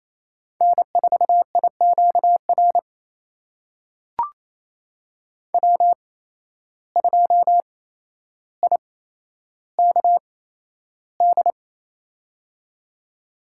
Morse code, heard as a keyed tone, about 28 wpm, spaced as for 12 wpm, 700 Hz. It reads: N4IQR W2IKD
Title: USA Call Signs - 5 Characters - 8x Character Spacing - 28wpm